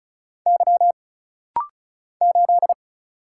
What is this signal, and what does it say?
Morse code, heard as a keyed tone, about 34 wpm, 700 Hz. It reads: Y 8
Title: Single Letter-Number Alt - Rapid-Fire 35wpm